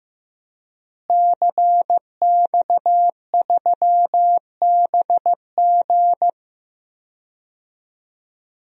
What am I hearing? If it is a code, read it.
Morse code, heard as a keyed tone, about 15 wpm, 700 Hz. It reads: CX3BG
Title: DXCC-100 Call Signs - 5 Characters - Repeated 2x - 15wpm